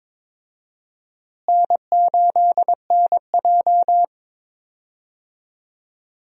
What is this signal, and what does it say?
Morse code, heard as a keyed tone, about 22 wpm, 700 Hz. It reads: N8NJ
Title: USA Call Signs - 4 Characters - Repeated 2x - 22wpm